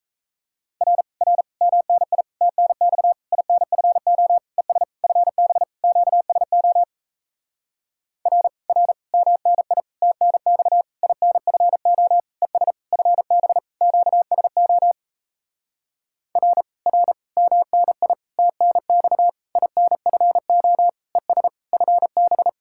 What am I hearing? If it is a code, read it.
Morse code, heard as a keyed tone, about 38 wpm, 700 Hz. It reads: R R MNI TNX INFO ES FB QSO R R MNI TNX INFO ES FB QSO R R MNI TNX INFO ES FB